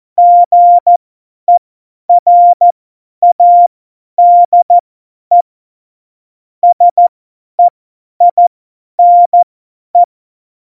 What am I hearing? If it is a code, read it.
Morse code, heard as a keyed tone, about 14 wpm, spaced as for 10 wpm, 700 Hz. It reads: GERADE SEINE
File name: Morsekurs / Witze / Witz 15. Mai 2018, 50 F70